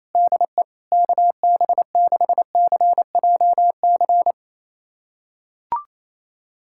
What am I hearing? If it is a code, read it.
Morse code, heard as a keyed tone, about 28 wpm, 700 Hz. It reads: DE KB6CJC
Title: USA Callsign DE Callsign - 6 Characters - 28wpm